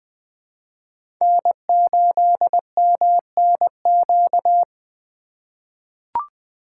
Morse code, heard as a keyed tone, about 20 wpm, 700 Hz. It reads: N8MNQ
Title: USA Call Signs - 5 Characters - 2x Character Spacing - 20wpm